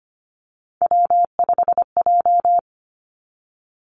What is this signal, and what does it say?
Morse code, heard as a keyed tone, about 25 wpm, 700 Hz. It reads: W5J
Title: Three Challenging Letter-Number Combinations 25wpm